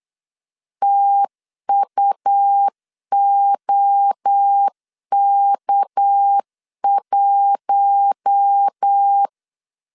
Morse code, heard as a keyed tone, about 8 wpm, 800 Hz. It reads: TUOK1
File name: LEZIONI / 6 PKX0 / 6 PK 8wpm 3003